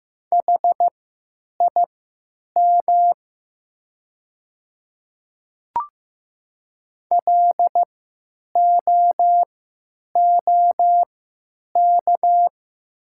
Morse code, heard as a keyed tone, about 15 wpm, spaced as for 9 wpm, 700 Hz. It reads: HIM LOOK
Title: 3x Character Spacing - Top 100 Words - 15wpm